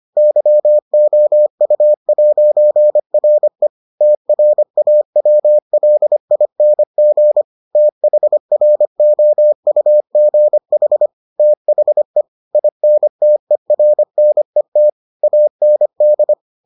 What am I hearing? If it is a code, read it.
Morse code, heard as a keyed tone, about 25 wpm, 600 Hz. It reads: YOU'RE TRAWLING THROUGH THE INTERNET AND